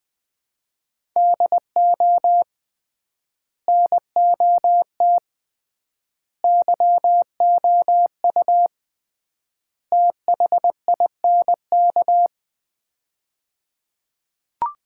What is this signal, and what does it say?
Morse code, heard as a keyed tone, about 20 wpm, 700 Hz. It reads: DO NOT YOU THINK
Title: 3x Word-Spacing - Sets of 4 Words - 20wpm